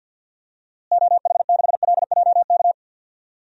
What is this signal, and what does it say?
Morse code, heard as a keyed tone, about 51 wpm, 700 Hz. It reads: OH6LJK
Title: DXCC-100 Call Signs - 6 Characters - Repeated 2x - 50wpm